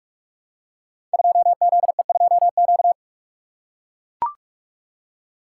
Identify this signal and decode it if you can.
Morse code, heard as a keyed tone, about 44 wpm, 700 Hz. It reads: 2ZE2Q
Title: Code Groups - 5 Letter:Numbers - 45wpm